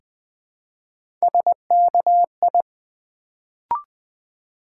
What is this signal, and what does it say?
Morse code, heard as a keyed tone, about 20 wpm, 700 Hz. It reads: SKI
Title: Build A Word - 1 Letter At A Time - 3-Letter Words - 20wpm